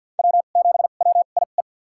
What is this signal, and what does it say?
Morse code, heard as a keyed tone, about 52 wpm, spaced as for 36 wpm, 700 Hz. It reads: W7WIE
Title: USA Call Signs - 2x Character Spacing - 50wpm